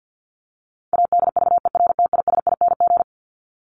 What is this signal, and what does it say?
Morse code, heard as a keyed tone, about 50 wpm, 700 Hz. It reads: ADVERTISING
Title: Top 751-1000 Words 50wpm